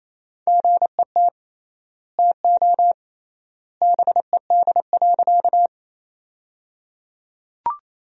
Morse code, heard as a keyed tone, about 28 wpm, 700 Hz. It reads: GET TO BED.
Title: 3x Word-Spacing - Sentences from Top 500 Words - 28wpm